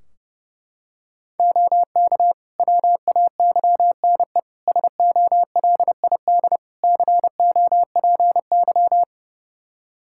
Morse code, heard as a keyed tone, about 30 wpm, 700 Hz. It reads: OK WAYNE SOLID COPY
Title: Practice QSO - Exchange 3 - More Info - 30wpm